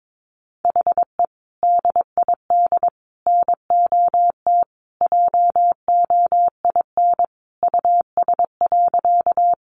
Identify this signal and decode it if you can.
Morse code, heard as a keyed tone, about 22 wpm, 700 Hz. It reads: HE DID NOT JOIN US.
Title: Sentences from Top 1400 Words 22wpm